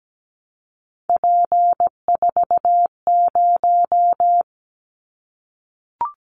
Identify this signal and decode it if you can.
Morse code, heard as a keyed tone, about 17 wpm, 700 Hz. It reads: P40
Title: Callsign Prefixes for Top 100 DXCC - 17wpm